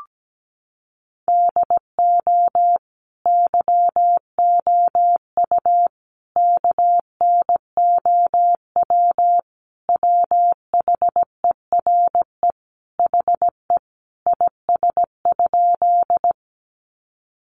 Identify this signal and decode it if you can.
Morse code, heard as a keyed tone, about 17 wpm, 700 Hz. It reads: DO YOU KNOW WHERE HE IS?
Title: Sentences from Top 300 Words 17wpm